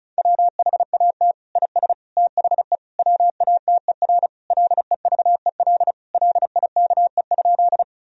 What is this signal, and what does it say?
Morse code, heard as a keyed tone, about 34 wpm, 700 Hz. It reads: WHAT IS THE WATER LEVEL LIKE?
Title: Sentences from Top 1000 Words - Review Part 4 - 35wpm